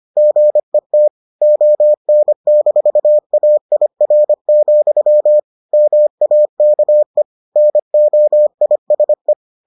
Morse code, heard as a keyed tone, about 25 wpm, 600 Hz. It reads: GET ON-AIR, MAKE NOISE